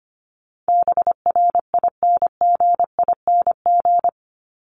Morse code, heard as a keyed tone, about 25 wpm, 700 Hz. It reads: BRINGING
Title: ING Suffix - Top 100 Words with Base Words 25wpm